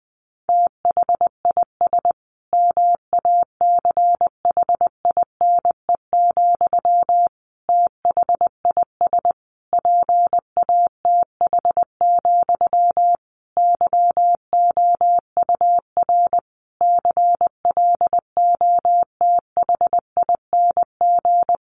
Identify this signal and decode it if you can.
Morse code, heard as a keyed tone, about 19 wpm, 700 Hz. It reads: THIS MACHINE, THIS PATH, YOUR CLOTHING
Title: A Sound Of Thunder 20wpm